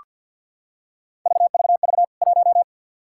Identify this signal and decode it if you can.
Morse code, heard as a keyed tone, about 46 wpm, 700 Hz. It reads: VVV 1
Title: K1USN SST QSO Practice 50wpm